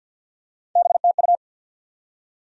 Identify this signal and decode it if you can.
Morse code, heard as a keyed tone, about 52 wpm, 700 Hz. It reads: BTU
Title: Top 180 QSO Elements 50wpm